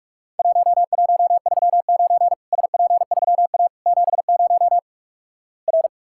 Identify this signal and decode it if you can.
Morse code, heard as a keyed tone, about 48 wpm, 700 Hz. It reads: 1139 SP3A 70 R
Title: Worked All Europe DX Contest - QTC Exchange - 45wpm